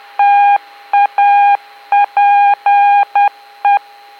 Morse code, heard as a keyed tone, about 10 wpm, 810 Hz. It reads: TAPE